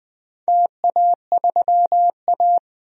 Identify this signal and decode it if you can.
Morse code, heard as a keyed tone, about 20 wpm, 700 Hz. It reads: TA3A
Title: DXCC-100 Call Signs - 4 Characters - 20wpm